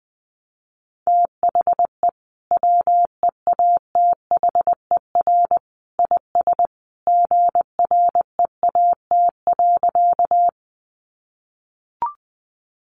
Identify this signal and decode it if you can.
Morse code, heard as a keyed tone, about 20 wpm, 700 Hz. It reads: THE WEATHER IS GREAT.
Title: Sentences from Top 800 Words 20wpm